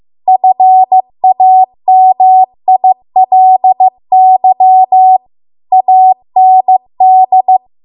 Morse code, heard as a keyed tone, about 15 wpm, 750 Hz. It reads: FAMILY AND
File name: Archive / 15 WPM / 15WPM